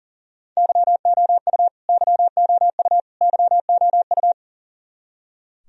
Morse code, heard as a keyed tone, about 40 wpm, 700 Hz. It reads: YOU YOU YOU
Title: Top 100 Words Repeated 40wpm